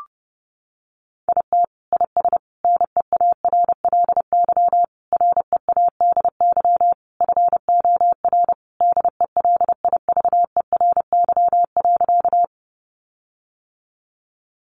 Morse code, heard as a keyed tone, about 30 wpm, 700 Hz. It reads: IT IS NEARLY READY FOR DELIVERY.
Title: Sentences from Top 1500 Words 30wpm